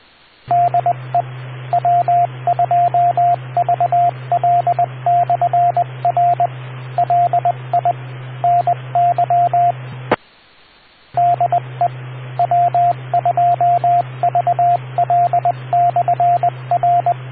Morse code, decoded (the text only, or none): DE W2VL/R LI NY DE W2VL/R